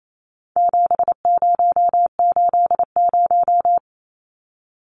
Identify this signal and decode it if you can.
Morse code, heard as a keyed tone, about 28 wpm, 700 Hz. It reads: 7080
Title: Numbers - 4-Digit 28wpm